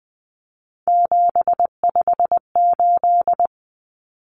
Morse code, heard as a keyed tone, about 20 wpm, 700 Hz. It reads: 758
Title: Common Cut Numbers - 3-Digit 20wpm